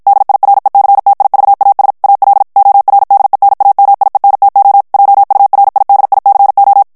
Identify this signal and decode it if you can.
Morse code, heard as a keyed tone, about 53 wpm, 800 Hz. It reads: DIRECTIVAS AL ORDENAMIENTO JURIDICO
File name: morse mp3 / ESPAGNOL TEXTE EN CLAIR 2 VITESSE 50 MOTS MINUTES